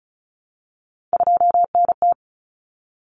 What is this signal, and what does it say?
Morse code, heard as a keyed tone, about 34 wpm, 700 Hz. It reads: 2NT
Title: Common Cut Numbers - 3-Digit 35wpm